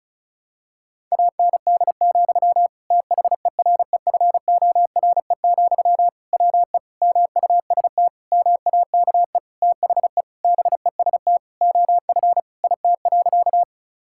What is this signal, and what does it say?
Morse code, heard as a keyed tone, about 36 wpm, 700 Hz. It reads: AND, THEREFORE, WE MUST MAKE THE BEST OF IT.
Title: Sentences from Top 1000 Words - Review Part 5 - 35wpm